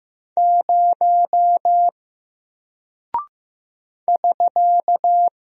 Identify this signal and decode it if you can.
Morse code, heard as a keyed tone, about 15 wpm, 700 Hz. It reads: 0 <SK>
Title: Single Character (Letter:Number:Punctuation:Prosign) 15wpm